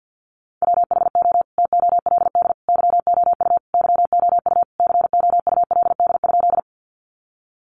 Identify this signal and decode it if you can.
Morse code, heard as a keyed tone, about 50 wpm, 700 Hz. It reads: WHO TOLD YOU YOU COULD?
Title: Sentences from Top 400 Words 50wpm